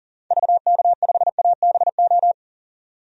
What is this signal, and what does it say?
Morse code, heard as a keyed tone, about 39 wpm, 700 Hz. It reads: VK5ABO
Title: DXCC-100 Call Signs - 6 Characters - 40wpm